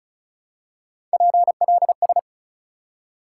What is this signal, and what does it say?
Morse code, heard as a keyed tone, about 34 wpm, 700 Hz. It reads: PLS
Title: Top 180 QSO Elements 35wpm